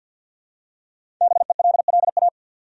Morse code, heard as a keyed tone, about 52 wpm, 700 Hz. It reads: BELLA
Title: USA Common Names - Max 5 Letters - 50wpm